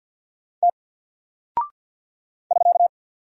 Morse code, heard as a keyed tone, about 48 wpm, 700 Hz. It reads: T <SK>
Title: Single Character (Letter:Number:Punctuation:Prosign) 50wpm